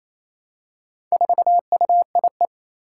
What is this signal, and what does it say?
Morse code, heard as a keyed tone, about 27 wpm, 700 Hz. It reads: 4UIE
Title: Four Challenging Letter-Number Combinations 28wpm